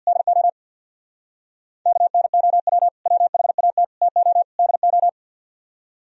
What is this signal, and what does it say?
Morse code, heard as a keyed tone, about 61 wpm, 700 Hz. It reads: DO KNOW WHAT TO DO
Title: Speed-Racing - Sets of 4 Words 50wpm